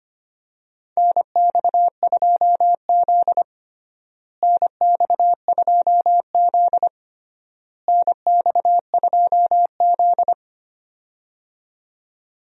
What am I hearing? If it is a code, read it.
Morse code, heard as a keyed tone, about 25 wpm, 700 Hz. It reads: NX2Z NX2Z NX2Z
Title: USA Call Signs - 4 Characters - Repeated 3x - 25wpm